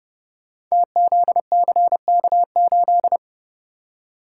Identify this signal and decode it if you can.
Morse code, heard as a keyed tone, about 30 wpm, 700 Hz. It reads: TZCK8
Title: Code Groups - 5 Letter:Numbers - 30wpm